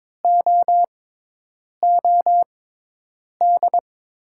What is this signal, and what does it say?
Morse code, heard as a keyed tone, about 22 wpm, spaced as for 8 wpm, 690 Hz. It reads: OOD
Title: 6x Character Spacing - Top 500 Words - 22wpm